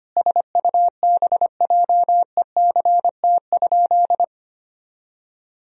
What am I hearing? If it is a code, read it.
Morse code, heard as a keyed tone, about 25 wpm, 700 Hz. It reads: SUBJECT?
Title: Sentences from Top 1000 Words - Review Part 4 - 25wpm